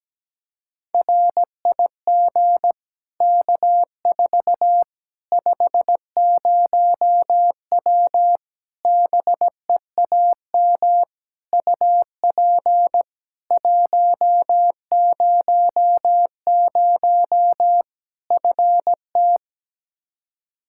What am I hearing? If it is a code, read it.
Morse code, heard as a keyed tone, about 17 wpm, 700 Hz. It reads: RIG K4 50W BEAM UP 100 FT
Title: Practice QSO - Exchange 2 - Info - 17wpm